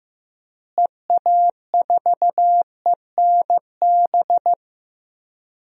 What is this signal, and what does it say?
Morse code, heard as a keyed tone, about 15 wpm, 700 Hz. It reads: EA4ENB
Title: DXCC-100 Call Signs - 6 Characters - Repeated 3x - 15wpm